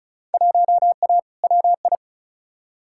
Morse code, heard as a keyed tone, about 34 wpm, 700 Hz. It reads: 1A WI
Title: Field Day Exchanges - 35wpm